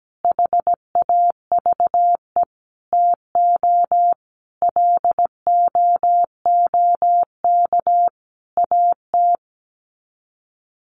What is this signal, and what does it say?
Morse code, heard as a keyed tone, about 17 wpm, 700 Hz. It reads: HAVE TO LOOK AT
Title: Sets of 5 Words Encore 17wpm